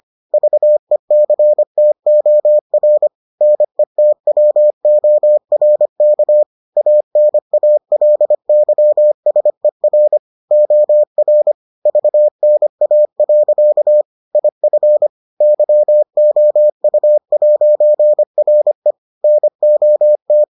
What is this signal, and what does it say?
Morse code, heard as a keyed tone, about 25 wpm, 600 Hz. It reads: VECTOR NETWORK ANALYSER OR VNA. IF YOU'RE NOT